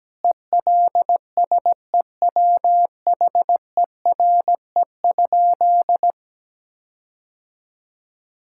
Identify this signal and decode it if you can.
Morse code, heard as a keyed tone, about 17 wpm, 700 Hz. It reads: ELSEWHERE?
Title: Sentences from Top 1600 Words 17wpm